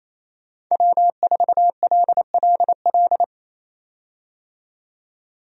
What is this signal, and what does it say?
Morse code, heard as a keyed tone, about 28 wpm, 700 Hz. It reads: W4LLL
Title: USA Call Signs - 5 Characters - 28wpm